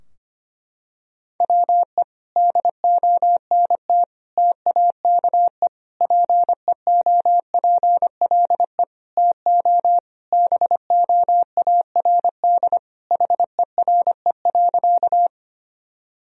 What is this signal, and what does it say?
Morse code, heard as a keyed tone, about 25 wpm, 700 Hz. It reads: WE DONT TAKE PEOPLE TO BOARD HERE.
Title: Sentences from Top 1000 Words - Review Part 4 - 25wpm